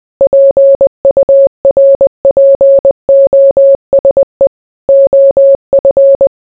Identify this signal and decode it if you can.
Morse code, heard as a keyed tone, about 20 wpm, 550 Hz. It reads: PURPOSE OF